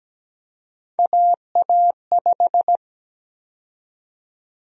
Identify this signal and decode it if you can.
Morse code, heard as a keyed tone, about 17 wpm, 700 Hz. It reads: AA5
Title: USA Call Sign Prefixes - Repeated 2x - 17wpm